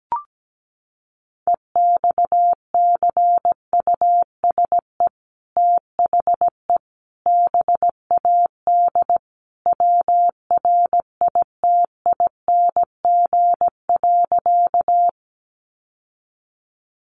Morse code, heard as a keyed tone, about 17 wpm, 700 Hz. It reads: EXCUSE THE BAD WRITING.
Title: Sentences from Top 800 Words 17wpm